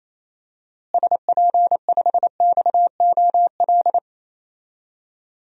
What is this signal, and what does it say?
Morse code, heard as a keyed tone, about 28 wpm, 700 Hz. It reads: SP5XOL
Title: DXCC-100 Call Signs - 6 Characters - Repeated 3x - 28wpm